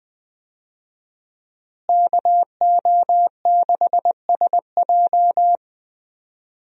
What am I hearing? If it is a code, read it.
Morse code, heard as a keyed tone, about 20 wpm, 700 Hz. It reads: KO6SJ
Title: USA Call Signs - 5 Characters - Repeated 2x - 20wpm